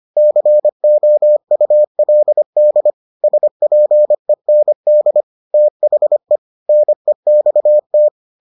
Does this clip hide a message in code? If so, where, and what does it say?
Yes: Morse code from 0.2 to 8.1 s, heard as a keyed tone, about 25 wpm, 600 Hz. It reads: COULD SPEND THE NEXT